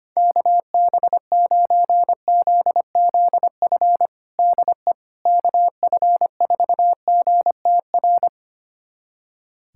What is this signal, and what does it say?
Morse code, heard as a keyed tone, about 25 wpm, 700 Hz. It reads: KB9ZZF DE KF4GTR
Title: USA Callsign DE Callsign - 6 Characters - 25wpm